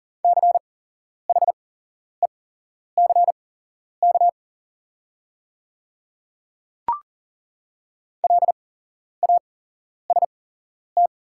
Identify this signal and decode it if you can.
Morse code, heard as a keyed tone, about 40 wpm, spaced as for 11 wpm, 690 Hz. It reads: CHECK LAST
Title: 8x Character Spacing - Top 500 Words - 40wpm